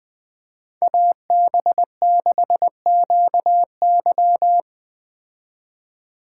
Morse code, heard as a keyed tone, about 20 wpm, 700 Hz. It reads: AB6QY